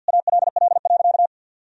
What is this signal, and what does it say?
Morse code, heard as a keyed tone, about 51 wpm, 700 Hz. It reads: ALL.
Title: Sentences from Top 1000 Words - Review Part 1 - 50wpm